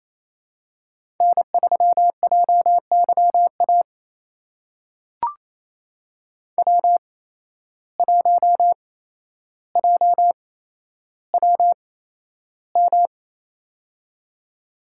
Morse code, heard as a keyed tone, about 28 wpm, spaced as for 12 wpm, 700 Hz. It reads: N3JYA W1JWM
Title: USA Call Signs - 5 Characters - 8x Character Spacing - 28wpm